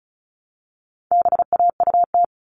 Morse code, heard as a keyed tone, about 34 wpm, 700 Hz. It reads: BAUT
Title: All Cut Numbers - 4-Digit 35wpm